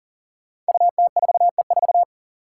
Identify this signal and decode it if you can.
Morse code, heard as a keyed tone, about 40 wpm, 700 Hz. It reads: UT4E4
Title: Five Challenging Letter-Number Combinations 40wpm